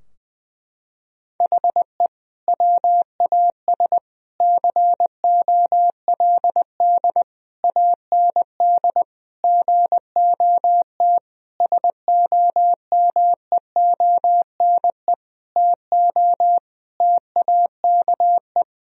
Morse code, heard as a keyed tone, about 20 wpm, 700 Hz. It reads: HE WAS COLD AND GOT SOMEONE TO TAKE
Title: Sentences from Top 1000 Words - Review Part 4 - 20wpm